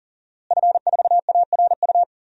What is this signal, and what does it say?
Morse code, heard as a keyed tone, about 40 wpm, 700 Hz. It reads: F4ARU
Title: DXCC-100 Call Signs - 5 Characters - Repeated 3x - 40wpm